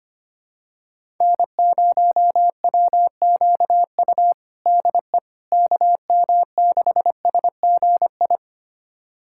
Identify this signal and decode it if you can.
Morse code, heard as a keyed tone, about 25 wpm, 700 Hz. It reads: N0WQU DE KM6SGI
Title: USA Callsign DE Callsign 25wpm